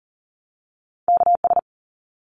Morse code, heard as a keyed tone, about 40 wpm, 700 Hz. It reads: KS
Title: ARRL Sections 40wpm